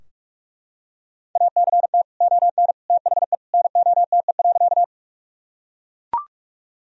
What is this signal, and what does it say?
Morse code, heard as a keyed tone, about 43 wpm, 700 Hz. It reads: ACT ON THE NOTE.
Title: Sentences from Top 1000 Words - Review Part 2 - 45wpm